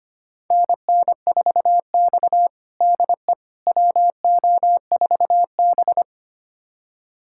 Morse code, heard as a keyed tone, about 25 wpm, 700 Hz. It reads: NN4X DE WO4B
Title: USA Callsign DE Callsign - 4 Characters - 25wpm